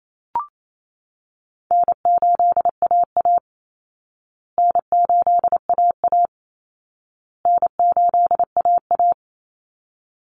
Morse code, heard as a keyed tone, about 28 wpm, 700 Hz. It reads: N8AA N8AA N8AA
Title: Call Signs Repeated 2x - 28wpm